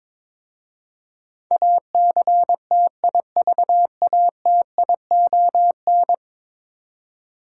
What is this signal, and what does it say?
Morse code, heard as a keyed tone, about 22 wpm, 700 Hz. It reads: ACTIVATION
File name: practice / 22wpm / 10 Letter Words - First 3 Letters Given - 22wpm